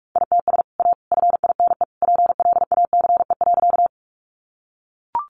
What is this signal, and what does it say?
Morse code, heard as a keyed tone, about 44 wpm, 700 Hz. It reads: ITS A FINE PLACE.
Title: Sentences from Top 600 Words 45wpm